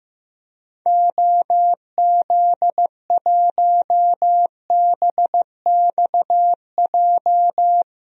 Morse code, heard as a keyed tone, about 15 wpm, 700 Hz. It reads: OZ1BXJ
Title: DXCC-100 Call Signs - 6 Characters - Repeated 3x - 15wpm